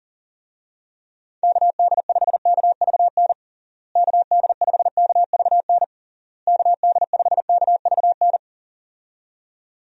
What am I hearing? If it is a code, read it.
Morse code, heard as a keyed tone, about 40 wpm, 700 Hz. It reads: KD5KVN KD5KVN KD5KVN
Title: USA Call Signs - 6 Characters - Repeated 3x - 40wpm